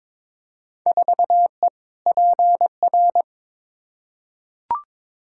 Morse code, heard as a keyed tone, about 22 wpm, 700 Hz. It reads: 4E PR
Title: Field Day Exchanges Encore - 22wpm